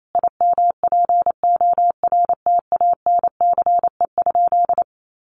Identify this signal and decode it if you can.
Morse code, heard as a keyed tone, about 28 wpm, 700 Hz. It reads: IMPORTANCE?
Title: Sentences from Top 1200 Words 28wpm